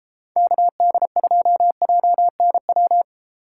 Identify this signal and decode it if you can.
Morse code, heard as a keyed tone, about 29 wpm, 700 Hz. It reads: KD2JNW